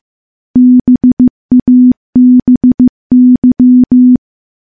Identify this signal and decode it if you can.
Morse code, heard as a keyed tone, about 15 wpm, 250 Hz. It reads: BABY
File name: morse-training-15WPM